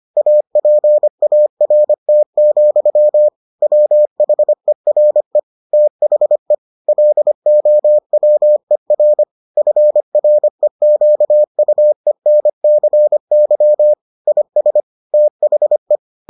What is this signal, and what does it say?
Morse code, heard as a keyed tone, about 25 wpm, 600 Hz. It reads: APART, WHERE THE LOWER FREQUENCY IS THE